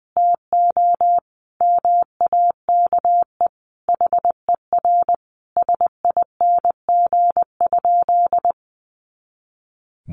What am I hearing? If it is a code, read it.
Morse code, heard as a keyed tone, about 20 wpm, 700 Hz. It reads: TO MAKE HER SING?
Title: Sentences from Top 400 Words 20wpm